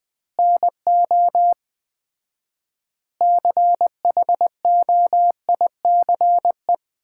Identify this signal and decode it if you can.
Morse code, heard as a keyed tone, about 20 wpm, 700 Hz. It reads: NO CHOICE
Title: 4x Word-Spacing - Sets of 5 Words - 20wpm